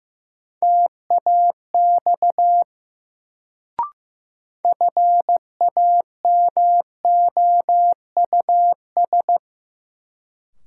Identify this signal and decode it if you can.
Morse code, heard as a keyed tone, about 15 wpm, 700 Hz. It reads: TAX FAMOUS
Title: Sentences from Top 1500 Words - New Word Drill Alt - 15wpm